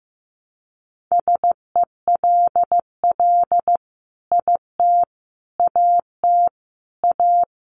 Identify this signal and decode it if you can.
Morse code, heard as a keyed tone, about 15 wpm, 700 Hz. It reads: SELL IT AT A
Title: Sentences from Top 1200 Words 15wpm